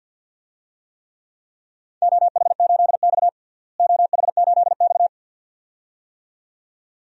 Morse code, heard as a keyed tone, about 51 wpm, 700 Hz. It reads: OH8X OH8X
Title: DXCC-100 Call Signs - 4 Characters - Repeated 2x - 50wpm